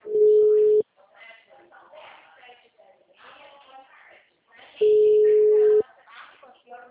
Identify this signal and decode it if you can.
Ringback tone (Europe, 425 Hz), with speech.
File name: out- -2010- - - .44221